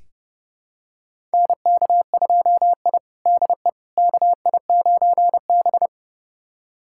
Morse code, heard as a keyed tone, about 30 wpm, 700 Hz. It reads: NK2I DE KI9B
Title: USA Callsign DE Callsign - 4 Characters - 30wpm